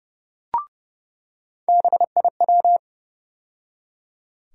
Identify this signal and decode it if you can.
Morse code, heard as a keyed tone, about 30 wpm, 700 Hz. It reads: BIW
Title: USA Call Sign Postfixes - 3 Letters 30wpm